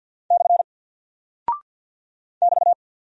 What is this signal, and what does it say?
Morse code, heard as a keyed tone, about 48 wpm, 700 Hz. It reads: / <BT>
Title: Single Character (Letter:Number:Punctuation:Prosign) Alt 50wpm